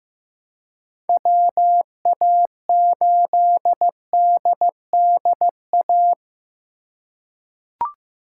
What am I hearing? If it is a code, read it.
Morse code, heard as a keyed tone, about 16 wpm, 700 Hz. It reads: WA8DDA
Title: USA Call Signs - 6 Characters - 3x Character Spacing - 15wpm